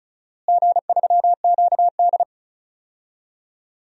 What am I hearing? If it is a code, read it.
Morse code, heard as a keyed tone, about 34 wpm, 700 Hz. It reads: G3QD
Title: DXCC-100 Call Signs - 4 Characters - Repeated 3x - 35wpm